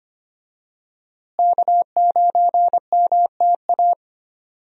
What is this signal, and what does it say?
Morse code, heard as a keyed tone, about 25 wpm, 700 Hz. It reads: K9MTA
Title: USA Call Signs - 5 Characters - Repeated 2x - 25wpm